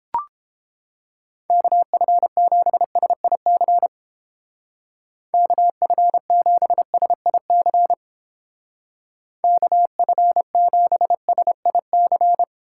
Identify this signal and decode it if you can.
Morse code, heard as a keyed tone, about 29 wpm, 690 Hz. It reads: KF7SIC KF7SIC KF7SIC